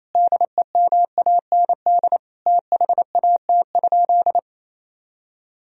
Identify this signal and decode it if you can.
Morse code, heard as a keyed tone, about 28 wpm, 700 Hz. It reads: DEMAND THAT?
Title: Sentences from Top 1000 Words - Review Part 4 - 28wpm